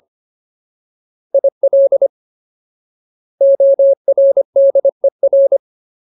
Morse code, heard as a keyed tone, about 25 wpm, 550 Hz. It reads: IL ORDER
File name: Ultimate Word List / 25WPM Ultimate